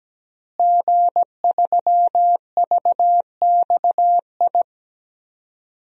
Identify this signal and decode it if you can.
Morse code, heard as a keyed tone, about 17 wpm, 700 Hz. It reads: G3VXI
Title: DXCC-100 Call Signs - 5 Characters - Repeated 2x - 17wpm